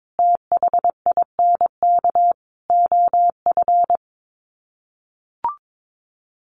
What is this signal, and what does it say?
Morse code, heard as a keyed tone, about 22 wpm, 700 Hz. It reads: THINK OF
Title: Sets of 5 Words Encore2 22wpm